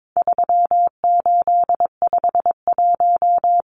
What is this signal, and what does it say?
Morse code, heard as a keyed tone, about 22 wpm, 700 Hz. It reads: 3851
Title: Numbers - 4-Digit 22wpm